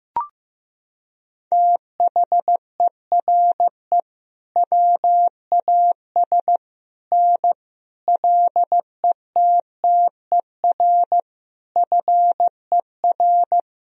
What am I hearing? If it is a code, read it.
Morse code, heard as a keyed tone, about 16 wpm, 700 Hz. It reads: THERE WAS N LETTER FER
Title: Sentences from Top 300 Words and Abbreviations 15wpm